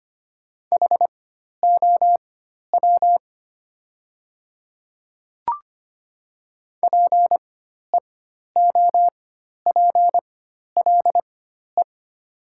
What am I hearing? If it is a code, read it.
Morse code, heard as a keyed tone, about 25 wpm, spaced as for 12 wpm, 700 Hz. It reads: HOW PEOPLE
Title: 4x Character Spacing - Top 100 Words - 25wpm